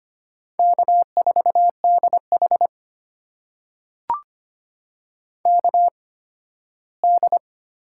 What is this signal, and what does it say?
Morse code, heard as a keyed tone, about 25 wpm, spaced as for 15 wpm, 700 Hz. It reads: K4DH KD
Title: USA Call Signs - 8x Character Spacing - 25wpm